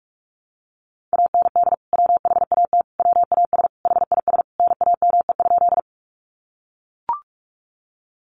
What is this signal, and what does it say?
Morse code, heard as a keyed tone, about 44 wpm, 700 Hz. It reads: AND WHAT WAS HIS NAME?
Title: Sentences from Top 200 Words 45wpm